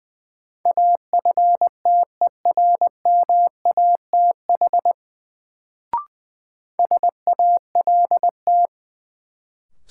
Morse code, heard as a keyed tone, about 20 wpm, 700 Hz. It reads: AFTERMATH SALT
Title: Sets of 5 Words - New Word Drill - 20wpm